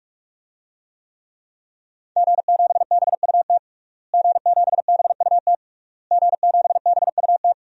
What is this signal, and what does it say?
Morse code, heard as a keyed tone, about 46 wpm, 700 Hz. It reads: G7BUT G7BUT G7BUT
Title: DXCC-100 Call Signs - 5 Characters - Repeated 3x - 45wpm